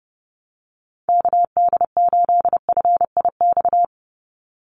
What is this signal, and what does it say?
Morse code, heard as a keyed tone, about 30 wpm, 700 Hz. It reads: KD8FIX
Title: USA Call Signs 30wpm